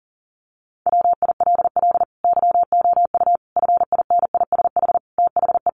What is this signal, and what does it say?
Morse code, heard as a keyed tone, about 40 wpm, 700 Hz. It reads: WILL YOU FINISH THE
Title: Sentences from Top 1300 Words 40wpm